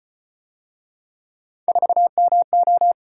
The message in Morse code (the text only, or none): 4MO